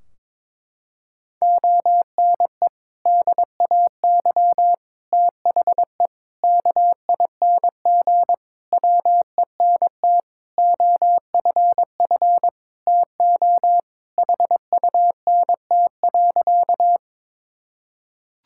Morse code, heard as a keyed tone, about 21 wpm, 700 Hz. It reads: ONE DAY THE KING WENT OFF TO HUNT.
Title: Sentences from Top 1600 Words 22wpm